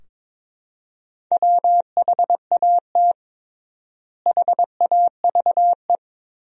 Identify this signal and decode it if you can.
Morse code, heard as a keyed tone, about 21 wpm, 700 Hz. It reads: WHAT HAVE
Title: 3x Word-Spacing - Sentences from Top 400 Words - 22wpm